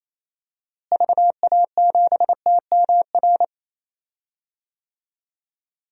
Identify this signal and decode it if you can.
Morse code, heard as a keyed tone, about 28 wpm, 700 Hz. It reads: VA7TMR
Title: DXCC-100 Call Signs - 6 Characters - Repeated 3x - 28wpm